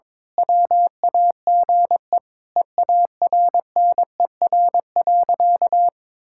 Morse code, heard as a keyed tone, about 22 wpm, 700 Hz. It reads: WAGE EARNER.
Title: Pangram Challenge v2 - 22wpm